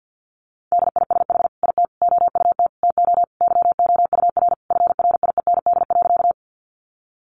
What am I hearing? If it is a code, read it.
Morse code, heard as a keyed tone, about 50 wpm, 700 Hz. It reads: DISH IT OUT TO YOUR FRIEND.
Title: Sentences from Top 1300 Words 50wpm